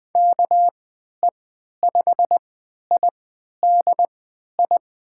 Morse code, heard as a keyed tone, about 21 wpm, spaced as for 12 wpm, 700 Hz. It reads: KE5IDI
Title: USA Call Signs - 6 Characters - 3x Character Spacing - 20wpm